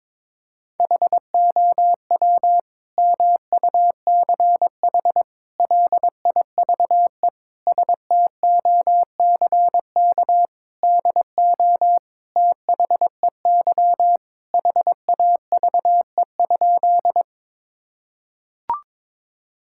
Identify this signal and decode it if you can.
Morse code, heard as a keyed tone, about 22 wpm, 700 Hz. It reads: HOW MUCH LIVE STOCK DO THEY HAVE?
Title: Sentences from Top 1000 Words - Review Part 5 - 22wpm